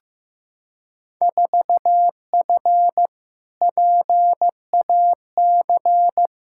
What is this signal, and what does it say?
Morse code, heard as a keyed tone, about 15 wpm, 700 Hz. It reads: 4F PAC
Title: Field Day Exchanges Encore - 15wpm